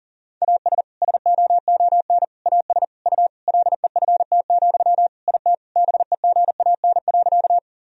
Morse code, heard as a keyed tone, about 40 wpm, 700 Hz. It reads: AS SOON AS U LEFT, IT BEGAN.
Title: Sentences from Top 400 Words and Abbreviations 40wpm